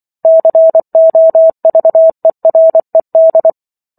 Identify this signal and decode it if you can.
Morse code, heard as a keyed tone, about 24 wpm, 650 Hz. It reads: COVERED